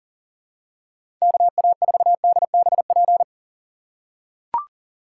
Morse code, heard as a keyed tone, about 40 wpm, 700 Hz. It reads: KA4DBP
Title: USA Call Signs - 6 Characters - 4x Character Spacing - 40wpm